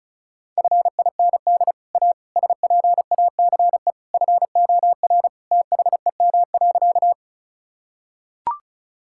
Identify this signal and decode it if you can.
Morse code, heard as a keyed tone, about 34 wpm, 700 Hz. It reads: FIND A SPACE FOR THEM.
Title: Sentences from Top 1000 Words - Review Part 5 - 35wpm